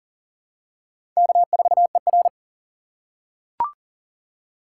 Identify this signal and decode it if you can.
Morse code, heard as a keyed tone, about 40 wpm, 700 Hz. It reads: K4ER
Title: USA Call Signs - 4 Characters - 4x Character Spacing - 40wpm